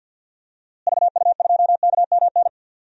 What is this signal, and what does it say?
Morse code, heard as a keyed tone, about 46 wpm, 700 Hz. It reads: VU2KMN
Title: DXCC-100 Call Signs - 6 Characters - Repeated 2x - 50wpm